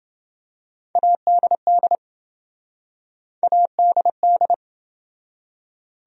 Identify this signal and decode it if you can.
Morse code, heard as a keyed tone, about 28 wpm, 700 Hz. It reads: ADD ADD